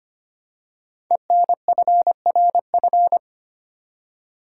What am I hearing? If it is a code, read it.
Morse code, heard as a keyed tone, about 25 wpm, 700 Hz. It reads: ENFRF
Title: Code Groups - 5 Letters - 25wpm